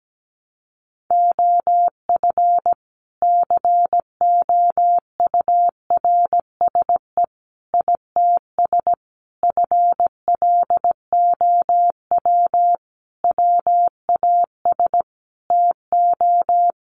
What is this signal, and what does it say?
Morse code, heard as a keyed tone, about 17 wpm, 700 Hz. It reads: OF COURSE ITS FLOW WAS TO
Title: Sentences from Top 1200 Words 17wpm